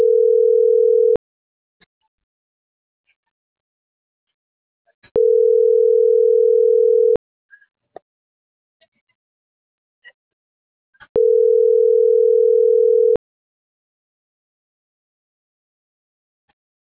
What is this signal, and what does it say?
Ringback tone (North America), 440 + 480 Hz.